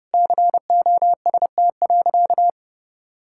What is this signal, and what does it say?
Morse code, heard as a keyed tone, about 30 wpm, 700 Hz. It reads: COST.
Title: Sentences from Top 1000 Words - Review Part 6 - 30wpm